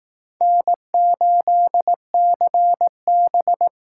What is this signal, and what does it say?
Morse code, heard as a keyed tone, about 19 wpm, 700 Hz. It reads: N8CB
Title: Speed Racing - USA 1x2 Call Signs 15wpm